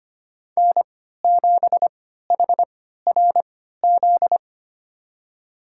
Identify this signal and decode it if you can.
Morse code, heard as a keyed tone, about 25 wpm, spaced as for 14 wpm, 700 Hz. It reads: N7HRZ
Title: USA Call Signs - 5 Characters - 3x Character Spacing - 25wpm